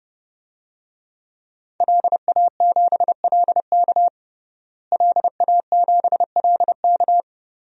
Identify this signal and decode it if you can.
Morse code, heard as a keyed tone, about 32 wpm, 700 Hz. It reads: LA7LK LA7LK
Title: DXCC-100 Call Signs - 5 Characters - Repeated 2x - 30wpm